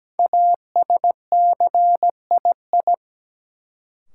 Morse code, heard as a keyed tone, about 17 wpm, 700 Hz. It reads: ASCII
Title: Sets of 5 Words - New Word Drill - 17wpm